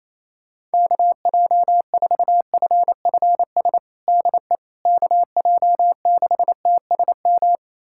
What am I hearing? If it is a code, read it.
Morse code, heard as a keyed tone, about 28 wpm, 700 Hz. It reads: KJ4FFS DE KJ6TSM
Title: USA Callsign DE Callsign - 6 Characters - 28wpm